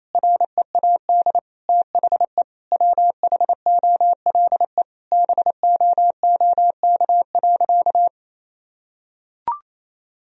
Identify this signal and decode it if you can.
Morse code, heard as a keyed tone, about 28 wpm, 700 Hz. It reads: READ THE WHOLE BOOK.
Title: Sentences from Top 1000 Words - Review Part 1 - 28wpm